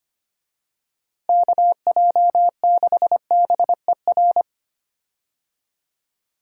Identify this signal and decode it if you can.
Morse code, heard as a keyed tone, about 25 wpm, 700 Hz. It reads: KJ6BER
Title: USA Call Signs - 6 Characters - Repeated 3x - 25wpm